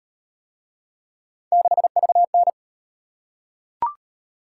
Morse code, heard as a keyed tone, about 38 wpm, 700 Hz. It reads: 6VN